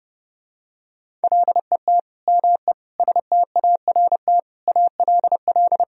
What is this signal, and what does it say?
Morse code, heard as a keyed tone, about 30 wpm, 700 Hz. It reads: LET ME START ALL
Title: Sentences from Top 1000 Words - Review Part 1 - 30wpm